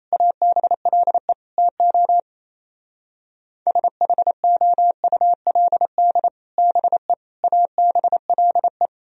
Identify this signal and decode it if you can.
Morse code, heard as a keyed tone, about 32 wpm, 700 Hz. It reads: ABLE TO SHOULD BE ABLE
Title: Speed-Racing - Sets of 4 Words 28wpm